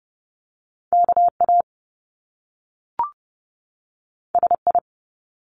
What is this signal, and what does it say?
Morse code, heard as a keyed tone, about 30 wpm, 700 Hz. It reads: KA SI
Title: Top Two Letter Combinations in Words 30wpm